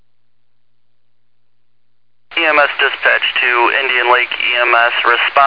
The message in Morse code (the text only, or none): none